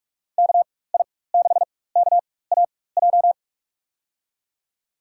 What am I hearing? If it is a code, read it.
Morse code, heard as a keyed tone, about 47 wpm, spaced as for 21 wpm, 700 Hz. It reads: KI6KAJ
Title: USA Call Signs - 6 Characters - 4x Character Spacing - 45wpm